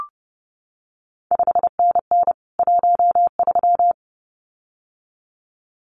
Morse code, heard as a keyed tone, about 30 wpm, 700 Hz. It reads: 5NN 13
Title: CQ WW DX Contest Exchange - 30wpm